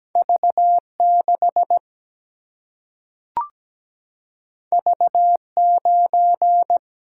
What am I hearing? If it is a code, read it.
Morse code, heard as a keyed tone, about 17 wpm, 700 Hz. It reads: V6 V9
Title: Single-Letter Single-Number 17wpm